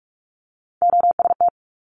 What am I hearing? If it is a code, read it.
Morse code, heard as a keyed tone, about 46 wpm, 700 Hz. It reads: OST
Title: Multi-letter Phonemes Quick Fire 45wpm